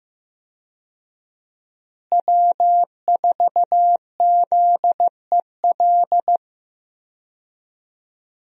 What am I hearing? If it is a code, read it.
Morse code, heard as a keyed tone, about 15 wpm, 700 Hz. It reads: W4ZEL
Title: USA Call Signs - 5 Characters - Repeated 2x - 15wpm